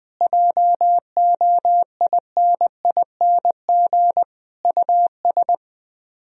Morse code, heard as a keyed tone, about 20 wpm, 700 Hz. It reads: JOINING US
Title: Speed-Racing - Sets of 5 Words 20wpm